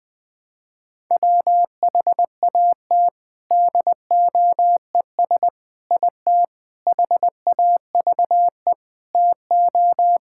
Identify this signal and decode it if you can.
Morse code, heard as a keyed tone, about 21 wpm, 700 Hz. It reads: WHAT DOES IT HAVE TO
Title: Sentences with 4-Word Phrases - Spoken Beginning - 20wpm